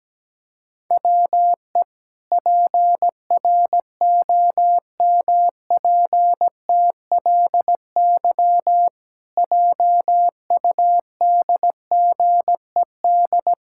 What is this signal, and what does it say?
Morse code, heard as a keyed tone, about 17 wpm, 700 Hz. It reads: WE PROMPTLY JUDGED
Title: Pangram Challenge v2 - 17wpm